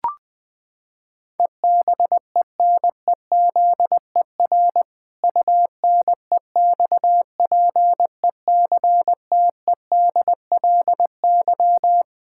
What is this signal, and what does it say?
Morse code, heard as a keyed tone, about 20 wpm, 700 Hz. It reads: EBENEZER UNEXPECTEDLY
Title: Pangram Challenge v2 - 20wpm